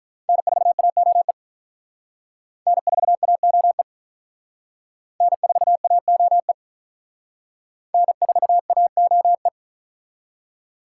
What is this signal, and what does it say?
Morse code, heard as a keyed tone, about 42 wpm, 700 Hz. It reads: N4AOE N4AOE N4AOE N4AOE
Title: Speed Racing - USA 1x3 Call Signs 35wpm